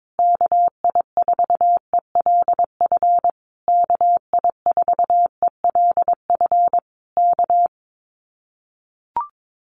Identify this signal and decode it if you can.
Morse code, heard as a keyed tone, about 22 wpm, 700 Hz. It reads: KI4ELF KI4ELF K
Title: Common QSO Phrases Encore 22wpm